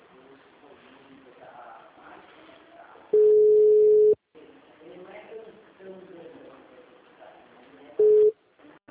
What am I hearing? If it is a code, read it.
Ringback tone (Europe, 425 Hz), with speech.